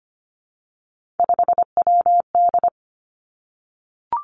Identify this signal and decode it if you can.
Morse code, heard as a keyed tone, about 25 wpm, 700 Hz. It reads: 5WD
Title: Three Challenging Letter-Number Combinations 25wpm